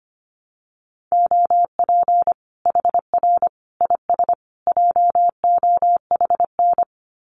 Morse code, heard as a keyed tone, about 25 wpm, 700 Hz. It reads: OP HR IS JOHN
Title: Common QSO Phrases 25wpm